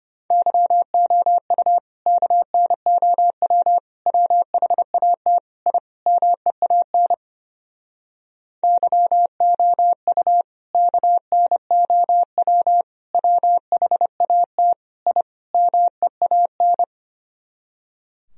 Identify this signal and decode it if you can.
Morse code, heard as a keyed tone, about 31 wpm, 700 Hz. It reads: YOU KNOW WHAT I MEAN YOU KNOW WHAT I MEAN
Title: Speed-Racing - Sets of 5 Words 25wpm